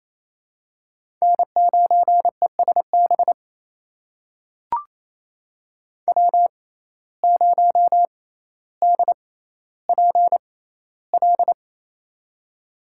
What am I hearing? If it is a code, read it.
Morse code, heard as a keyed tone, about 28 wpm, spaced as for 14 wpm, 700 Hz. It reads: N9ESB W0DPL
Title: USA Call Signs - 5 Characters - 6x Character Spacing - 28wpm